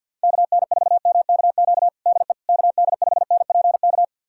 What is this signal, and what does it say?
Morse code, heard as a keyed tone, about 51 wpm, 700 Hz. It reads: KN4MKQ DE KD5NPK
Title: USA Callsign DE Callsign - 6 Characters - 50wpm